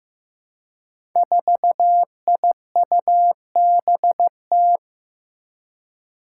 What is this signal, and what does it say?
Morse code, heard as a keyed tone, about 15 wpm, 700 Hz. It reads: 4IUBT
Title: Five Challenging Letter-Number Combinations 15wpm